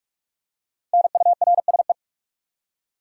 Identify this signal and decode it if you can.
Morse code, heard as a keyed tone, about 46 wpm, 700 Hz. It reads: NURSE
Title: Five Letter Words - 45wpm